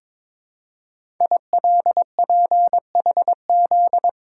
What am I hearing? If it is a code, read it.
Morse code, heard as a keyed tone, about 23 wpm, 700 Hz. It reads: ILPHZ
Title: Code Groups - 5 Letter:Numbers - 22wpm